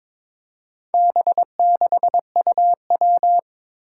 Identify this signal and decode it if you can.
Morse code, heard as a keyed tone, about 22 wpm, 700 Hz. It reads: B6UW
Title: Four Challenging Letter-Number Combinations 22wpm